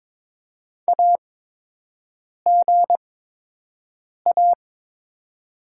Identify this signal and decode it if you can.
Morse code, heard as a keyed tone, about 21 wpm, spaced as for 6 wpm, 700 Hz. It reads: AGA
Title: 8x Character Spacing - Top 200 Words - 22wpm